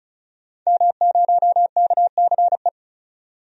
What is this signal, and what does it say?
Morse code, heard as a keyed tone, about 34 wpm, 700 Hz. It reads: M0KCE
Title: DXCC-100 Call Signs - 5 Characters - 35wpm